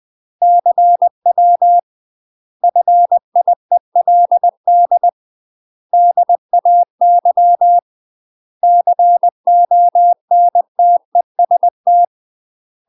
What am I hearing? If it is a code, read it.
Morse code, heard as a keyed tone, about 20 wpm, 700 Hz. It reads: CW FIELD DAY CONTEST
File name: UK MASTER QSOs TRAINING / 20WPM / UK MASTER QSOs 0036